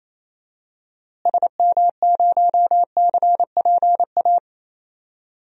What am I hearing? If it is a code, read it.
Morse code, heard as a keyed tone, about 28 wpm, 700 Hz. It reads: SM0CPA
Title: DXCC-100 Call Signs - 6 Characters - Repeated 3x - 28wpm